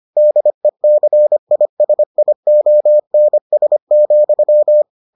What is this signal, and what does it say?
Morse code, heard as a keyed tone, about 25 wpm, 600 Hz. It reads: DECISIONS,